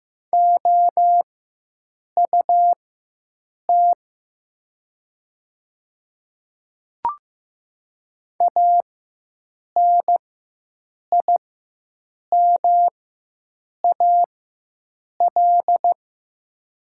Morse code, heard as a keyed tone, about 15 wpm, spaced as for 7 wpm, 700 Hz. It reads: OUT ANIMAL
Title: 4x Character Spacing - Top 200 Words - 15wpm